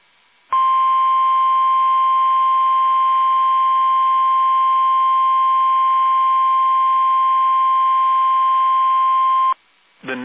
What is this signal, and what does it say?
Emergency Alert System (SAME) tones (the 1050 Hz attention tone). It is heard from 0.5 to 9.6 s.
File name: eas / KUSO / KUSO